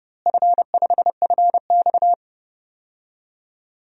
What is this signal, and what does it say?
Morse code, heard as a keyed tone, about 30 wpm, 700 Hz. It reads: F5FX
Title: DXCC-100 Call Signs - 4 Characters - Repeated 3x - 30wpm